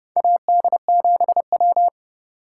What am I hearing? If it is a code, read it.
Morse code, heard as a keyed tone, about 30 wpm, 700 Hz. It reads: AD7W
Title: USA Call Signs - 4 Characters - 4x Character Spacing - 30wpm